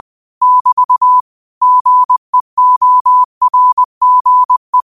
Morse code, heard as a keyed tone, about 20 wpm, 1000 Hz. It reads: <BT> GEORGE
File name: qotdfiles / 20wpm